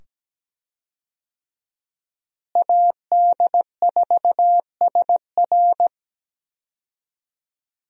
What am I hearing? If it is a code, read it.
Morse code, heard as a keyed tone, about 17 wpm, 700 Hz. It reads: AD4SR
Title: USA Call Signs - 5 Characters - Repeated 2x - 17wpm